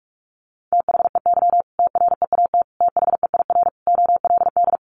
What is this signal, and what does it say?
Morse code, heard as a keyed tone, about 44 wpm, 700 Hz. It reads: THEY TREAT THEIR OLD
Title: Sentences from Top 1300 Words 45wpm